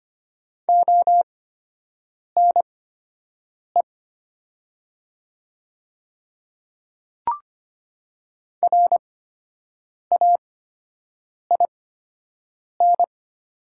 Morse code, heard as a keyed tone, about 25 wpm, spaced as for 7 wpm, 700 Hz. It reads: ONE RAIN
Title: 8x Character Spacing - Top 300 Words - 25wpm